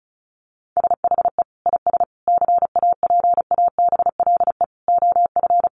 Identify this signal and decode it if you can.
Morse code, heard as a keyed tone, about 34 wpm, 700 Hz. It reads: SHE IS CAPABLE OF
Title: Sentences from Top 1200 Words 35wpm